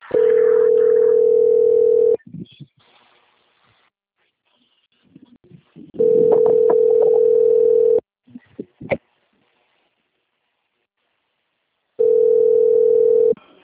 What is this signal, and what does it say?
Ringback tone (North America), 440 + 480 Hz.
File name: out- - - - - .33568